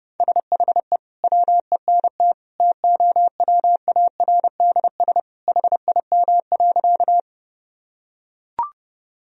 Morse code, heard as a keyed tone, about 30 wpm, 700 Hz. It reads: SHE WENT TOWARDS HIM.
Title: Sentences from Top 1000 Words - Review Part 2 - 30wpm